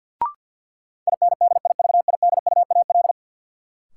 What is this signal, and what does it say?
Morse code, heard as a keyed tone, about 48 wpm, 700 Hz. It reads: INDIVIDUAL
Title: Sentences from Top 1400 Words - New Word Drill Alt - 50wpm